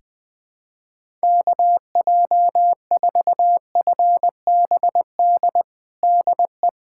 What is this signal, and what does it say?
Morse code, heard as a keyed tone, about 20 wpm, 700 Hz. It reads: KJ4FBD DE
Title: USA Callsign DE Callsign - 6 Characters - 20wpm